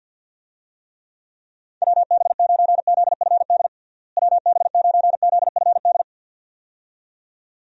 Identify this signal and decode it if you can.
Morse code, heard as a keyed tone, about 49 wpm, 700 Hz. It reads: WB9ZFD WB9ZFD
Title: USA Call Signs - 6 Characters - Repeated 2x - 50wpm